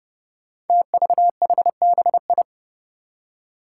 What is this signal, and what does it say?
Morse code, heard as a keyed tone, about 29 wpm, 700 Hz. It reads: TVHBI
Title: Five Challenging Letter-Number Combinations 30wpm